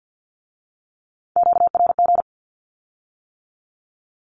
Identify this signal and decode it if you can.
Morse code, heard as a keyed tone, about 50 wpm, 700 Hz. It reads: QRG
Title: Q-Codes with Spoken Meaning - 50wpm